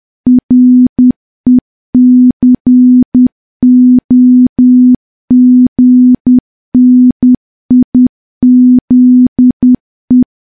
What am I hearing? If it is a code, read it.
Morse code, heard as a keyed tone, about 10 wpm, 250 Hz. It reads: RECOGNIZE
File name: morse / morse-training-10WPM